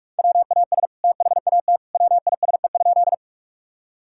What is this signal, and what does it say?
Morse code, heard as a keyed tone, about 46 wpm, 700 Hz. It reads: WAS THAT WISE?
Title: Sentences from Top 1600 Words 45wpm